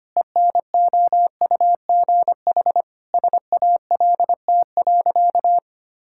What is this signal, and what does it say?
Morse code, heard as a keyed tone, about 25 wpm, 700 Hz. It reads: ENOUGH SALT.
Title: Sentences from Top 1000 Words - Review Part 5 - 25wpm